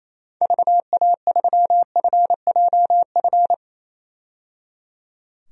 Morse code, heard as a keyed tone, about 28 wpm, 700 Hz. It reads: VA3FJF
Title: DXCC-100 Call Signs - 6 Characters - 28wpm